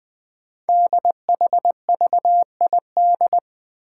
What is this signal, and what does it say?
Morse code, heard as a keyed tone, about 20 wpm, 700 Hz. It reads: DHVID
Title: Code Groups - 5 Letters - 20wpm